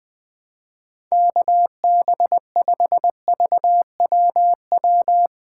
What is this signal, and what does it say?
Morse code, heard as a keyed tone, about 20 wpm, 700 Hz. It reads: KB5VWW
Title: USA Call Signs - 6x Character Spacing - 20wpm